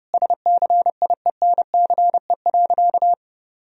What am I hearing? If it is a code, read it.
Morse code, heard as a keyed tone, about 30 wpm, 700 Hz. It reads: SCIENCE.
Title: Sentences from Top 1000 Words - Review Part 2 - 30wpm